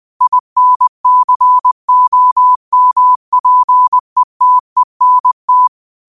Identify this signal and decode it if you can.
Morse code, heard as a keyed tone, about 20 wpm, 1000 Hz. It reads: INCOMPETENT
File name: qotdfiles / 20wpm